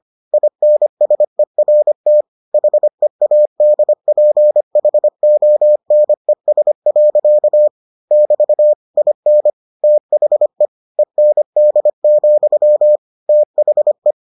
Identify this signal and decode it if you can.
Morse code, heard as a keyed tone, about 25 wpm, 600 Hz. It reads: INSERT HEADPHONES. <BT> IN THE END, THE